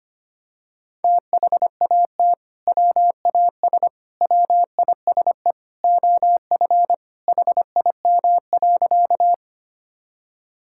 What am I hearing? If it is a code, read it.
Morse code, heard as a keyed tone, about 25 wpm, 700 Hz. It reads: THAT WAS WISE OF HIM.
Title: Sentences from Top 1600 Words 25wpm